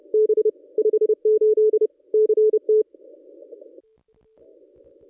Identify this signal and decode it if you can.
Morse code, heard as a keyed tone, about 31 wpm, 430 Hz. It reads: B 58 CT
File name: contests / ss-cw-k5tr-op-n5rz -1200